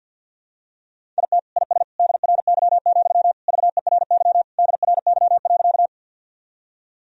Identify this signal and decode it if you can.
Morse code, heard as a keyed tone, about 52 wpm, 700 Hz. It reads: IT IS DRY, VERY DRY.
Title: Sentences from Top 1000 Words - Review Part 3 - 50wpm